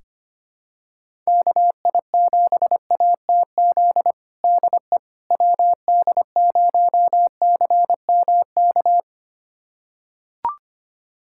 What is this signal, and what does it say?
Morse code, heard as a keyed tone, about 25 wpm, 700 Hz. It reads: KI7ATZ DE WD0CMK
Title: USA Callsign DE Callsign - 6 Characters - 25wpm